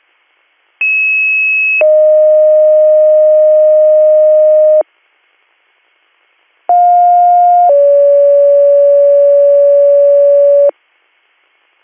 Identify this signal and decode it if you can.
Two-tone page, heard as a one-second tone, then a three-second tone, about 2600 then 620 Hz, about 710 then 570 Hz. It is heard from 0.8 to 10.7 s.